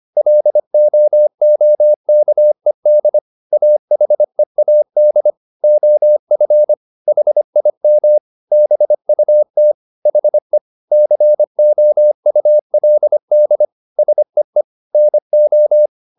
Morse code, heard as a keyed tone, about 25 wpm, 600 Hz. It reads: LOOKED AHEAD OF HIM BUT HE COULD SEE NO